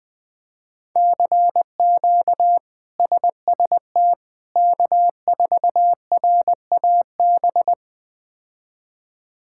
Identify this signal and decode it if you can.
Morse code, heard as a keyed tone, about 20 wpm, 700 Hz. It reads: CQ SST K4RAB
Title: K1USN SST QSO Practice - 20wpm